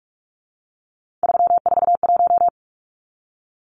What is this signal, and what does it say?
Morse code, heard as a keyed tone, about 44 wpm, 700 Hz. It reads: 341
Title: Numbers - 3-Digit 45wpm